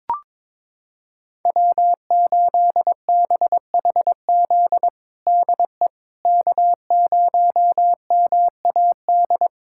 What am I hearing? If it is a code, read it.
Morse code, heard as a keyed tone, about 21 wpm, 700 Hz. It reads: W8BHZ DE K0MAD
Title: USA Callsign DE Callsign - 5 Characters - 22wpm